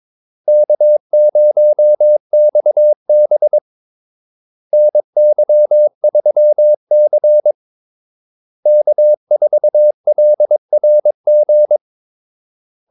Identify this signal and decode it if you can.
Morse code, heard as a keyed tone, about 22 wpm, 600 Hz. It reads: K0XB NY3C K4LRG
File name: Callsigns-22wpm